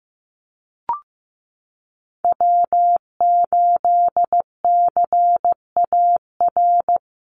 Morse code, heard as a keyed tone, about 15 wpm, 700 Hz. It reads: W8CAR
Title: Call Signs Repeated 3x - 15wpm